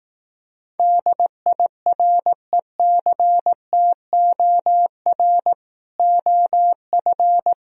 Morse code, heard as a keyed tone, about 18 wpm, 700 Hz. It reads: DIRECTOR OF
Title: Speed-Racing - Sets of 3 Words 15wpm